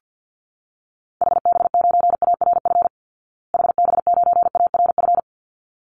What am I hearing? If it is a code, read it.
Morse code, heard as a keyed tone, about 49 wpm, 700 Hz. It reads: HB9ARF HB9ARF
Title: Call Signs Repeated 3x - 50wpm